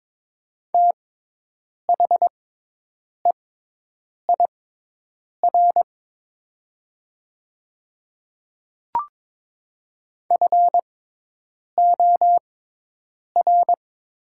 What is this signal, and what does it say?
Morse code, heard as a keyed tone, about 23 wpm, spaced as for 8 wpm, 700 Hz. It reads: THEIR FOR
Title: 6x Character Spacing - Top 100 Words - 22wpm